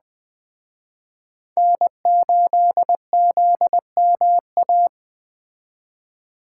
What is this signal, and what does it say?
Morse code, heard as a keyed tone, about 20 wpm, 700 Hz. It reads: N8ZMA